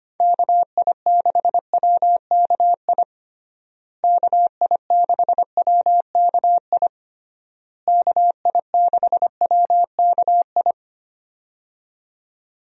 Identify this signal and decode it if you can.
Morse code, heard as a keyed tone, about 25 wpm, 700 Hz. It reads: KI6WKI KI6WKI KI6WKI
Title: USA Call Signs - 6 Characters - Repeated 3x - 25wpm